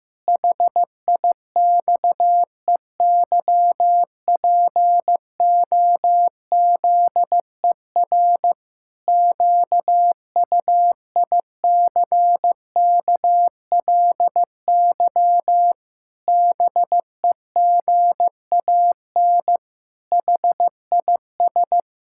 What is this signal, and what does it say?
Morse code, heard as a keyed tone, about 15 wpm, 700 Hz. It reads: HIXEYPOZER QUICKLY BEGAN HIS
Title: Pangram Challenge v2 - 15wpm